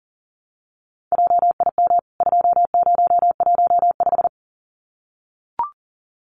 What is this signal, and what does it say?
Morse code, heard as a keyed tone, about 40 wpm, 700 Hz. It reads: JIM 2015
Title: CWT Exchange 40wpm